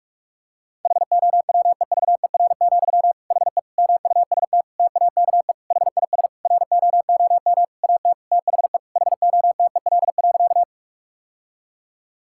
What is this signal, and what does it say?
Morse code, heard as a keyed tone, about 41 wpm, 700 Hz. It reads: HOWEVER, HE MUST TAKE HIS ROOM AT THE HOTEL.
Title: Sentences from Top 1600 Words 45wpm